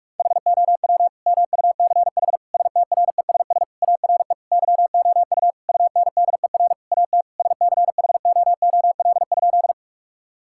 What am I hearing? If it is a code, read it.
Morse code, heard as a keyed tone, about 46 wpm, 700 Hz. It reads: HOW MUCH STRESS ARE YOU UNDER AT SCHOOL?
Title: Sentences from Top 1000 Words - Review Part 6 - 45wpm